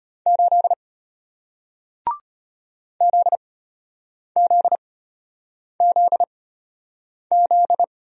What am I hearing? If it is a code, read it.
Morse code, heard as a keyed tone, about 34 wpm, spaced as for 8 wpm, 700 Hz. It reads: 8 ZZZZ